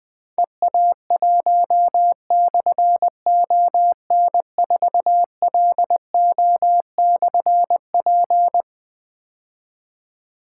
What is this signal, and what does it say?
Morse code, heard as a keyed tone, about 20 wpm, 700 Hz. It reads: EA1/ON4LO/P
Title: Difficult Call Signs 20wpm